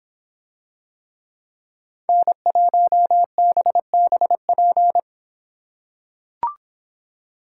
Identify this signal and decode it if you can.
Morse code, heard as a keyed tone, about 26 wpm, 700 Hz. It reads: N1BBP